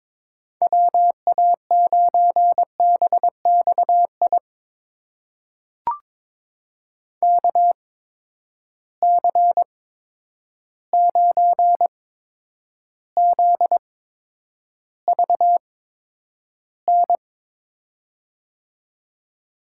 Morse code, heard as a keyed tone, about 22 wpm, spaced as for 9 wpm, 700 Hz. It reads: WA9BXI KC9ZVN
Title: USA Call Signs - 6 Characters - 8x Character Spacing - 22wpm